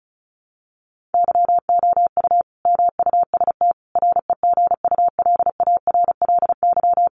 Morse code, heard as a keyed tone, about 34 wpm, 700 Hz. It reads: YOU MUST REGULARLY
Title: Sentences from Top 1400 Words 35wpm